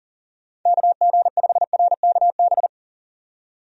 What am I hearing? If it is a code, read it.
Morse code, heard as a keyed tone, about 40 wpm, 700 Hz. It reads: KG5RKB
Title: USA Call Signs - 6 Characters - 40wpm